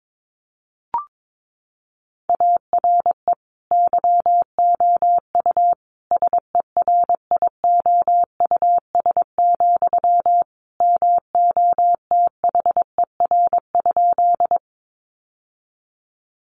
Morse code, heard as a keyed tone, about 22 wpm, 700 Hz. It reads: ARE YOU SERIOUS, MOTHER?
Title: Sentences from Top 1100 Words 22wpm